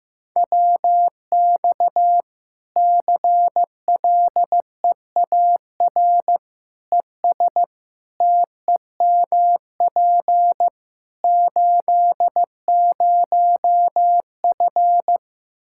Morse code, heard as a keyed tone, about 16 wpm, 700 Hz. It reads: WX CLEAR ES TEMP 80F
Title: K1USN SST QSO Practice 15wpm